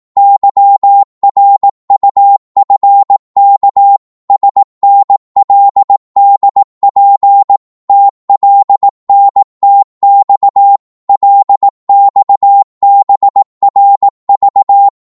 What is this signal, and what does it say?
Morse code, heard as a keyed tone, about 18 wpm, 800 Hz. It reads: YRUFK SNLDP TLNTX LXBRV
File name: Morse / ltr-18wpm